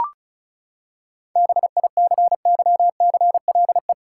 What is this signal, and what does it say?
Morse code, heard as a keyed tone, about 36 wpm, 700 Hz. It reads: BICYCLE
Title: Seven Letter Words - 35wpm